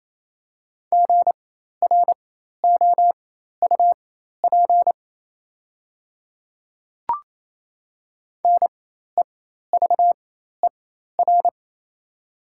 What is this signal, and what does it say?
Morse code, heard as a keyed tone, about 28 wpm, spaced as for 13 wpm, 700 Hz. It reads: GROUP NEVER
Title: 4x Character Spacing - Top 300 Words - 28wpm